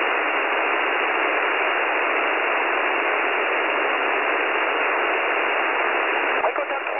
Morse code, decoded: none

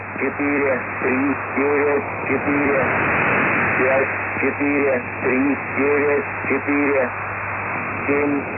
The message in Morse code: none

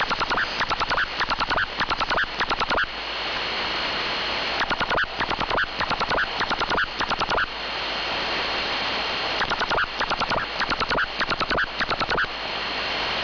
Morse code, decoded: none

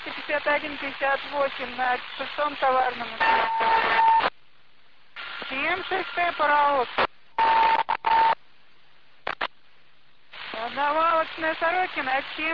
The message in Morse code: none

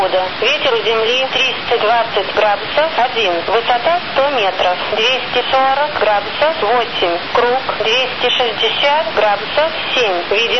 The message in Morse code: none

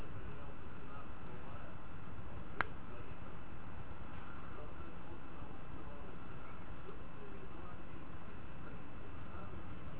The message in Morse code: none

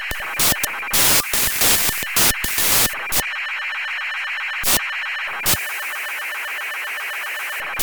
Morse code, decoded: none

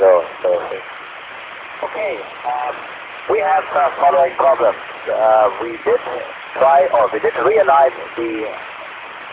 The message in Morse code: none

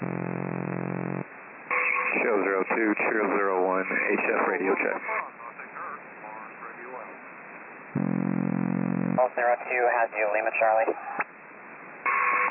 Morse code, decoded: none

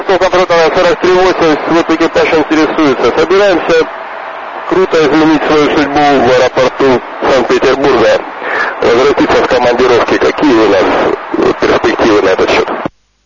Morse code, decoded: none

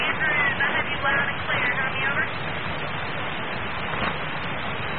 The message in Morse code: none

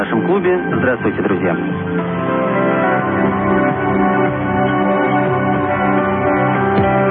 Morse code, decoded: none

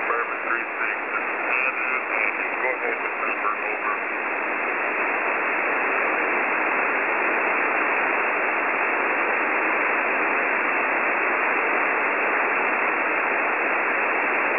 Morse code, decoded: none

